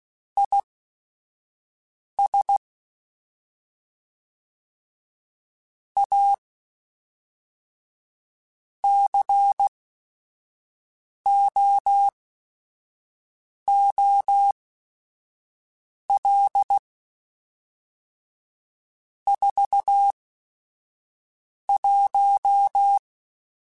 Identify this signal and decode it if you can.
Morse code, heard as a keyed tone, about 16 wpm, spaced as for 5 wpm, 800 Hz. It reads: IS A COOL 41
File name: Weather Sets / Weather Sets-mp3s / Weather 4 14 @5